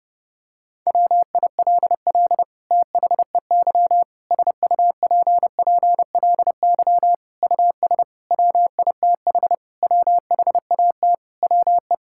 Morse code, heard as a keyed tone, about 32 wpm, 700 Hz. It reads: WILL THEY SUPPLY US WITH WHAT WE
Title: Sentences from Top 1000 Words - Review Part 5 - 30wpm